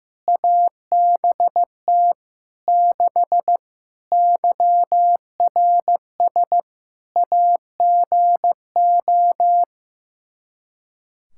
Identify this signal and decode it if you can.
Morse code, heard as a keyed tone, about 16 wpm, 700 Hz. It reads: ABT 6 YRS AGO
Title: Practice QSO - Exchange 3 - More Info - 15wpm